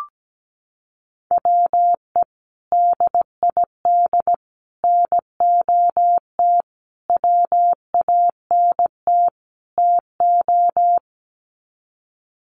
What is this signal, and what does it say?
Morse code, heard as a keyed tone, about 17 wpm, 700 Hz. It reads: WE DID NOT WANT TO
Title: Sets of 5 Words Encore 17wpm